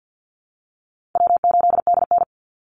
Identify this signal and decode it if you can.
Morse code, heard as a keyed tone, about 49 wpm, 700 Hz. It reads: W8DN
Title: Easy Call Signs Repeated 2x - 50wpm